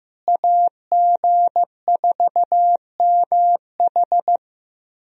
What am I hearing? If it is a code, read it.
Morse code, heard as a keyed tone, about 16 wpm, 700 Hz. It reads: AG4MH
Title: USA Call Signs - 5 Characters - Repeated 3x - 15wpm